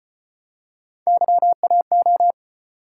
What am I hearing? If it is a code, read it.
Morse code, heard as a keyed tone, about 37 wpm, 700 Hz. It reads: YAO